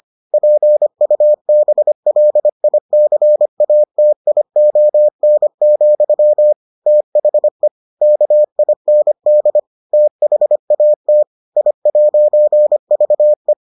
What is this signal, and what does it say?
Morse code, heard as a keyed tone, about 25 wpm, 600 Hz. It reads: PUBLICATION, THE KIND THAT I'VE